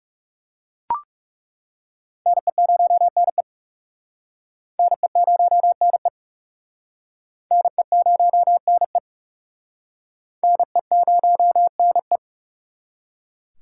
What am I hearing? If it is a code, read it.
Morse code, heard as a keyed tone, about 37 wpm, 700 Hz. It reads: NE0NE NE0NE NE0NE NE0NE
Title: Speed Racing - USA Call Signs 30wpm